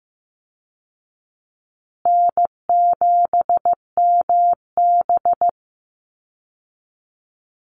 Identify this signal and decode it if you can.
Morse code, heard as a keyed tone, about 15 wpm, 700 Hz. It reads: N7MB
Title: Easy Call Signs Repeated 3x - 15wpm